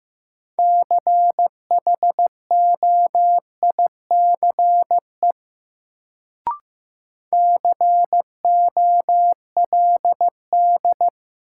Morse code, heard as a keyed tone, about 15 wpm, 700 Hz. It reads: CHOICE COLD
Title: Sets of 5 Words - New Word Drill - 15wpm